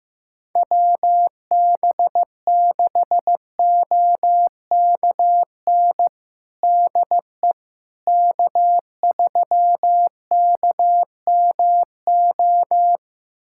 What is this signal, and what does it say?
Morse code, heard as a keyed tone, about 15 wpm, 700 Hz. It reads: WB6OKN DE K3KMO
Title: USA Callsign DE Callsign 15wpm